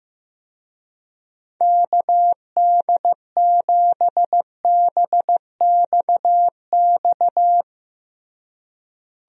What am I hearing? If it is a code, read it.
Morse code, heard as a keyed tone, about 15 wpm, 700 Hz. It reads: KD7BXX
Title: USA Call Signs - 6 Characters - Repeated 3x - 15wpm